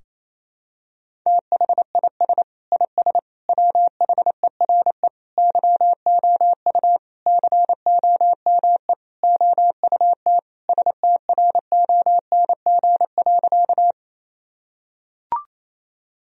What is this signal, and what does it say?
Morse code, heard as a keyed tone, about 28 wpm, 700 Hz. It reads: THIS IS WHERE YOU COME OUT STRONG.
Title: Sentences from Top 1000 Words - Review Part 3 - 28wpm